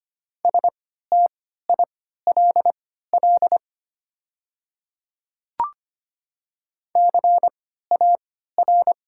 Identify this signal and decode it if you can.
Morse code, heard as a keyed tone, about 25 wpm, spaced as for 14 wpm, 690 Hz. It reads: STILL CAR
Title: 3x Character Spacing - Top 300 Words - 25wpm